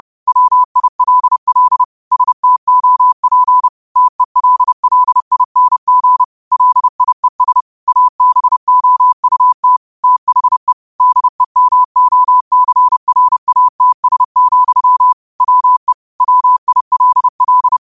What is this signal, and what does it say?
Morse code, heard as a keyed tone, about 30 wpm, 1000 Hz. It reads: WILL STOP TELLING LIES ABOUT THE DEMOCRATS, WE WILL
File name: qotdfiles / 30wpm